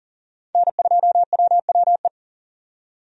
Morse code, heard as a keyed tone, about 40 wpm, 700 Hz. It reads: N2WWE
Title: USA Call Signs - 5 Characters - 40wpm